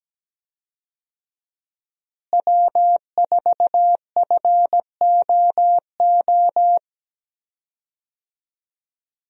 Morse code, heard as a keyed tone, about 17 wpm, 700 Hz. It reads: W4FOO
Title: USA Call Signs - 5 Characters - Repeated 3x - 17wpm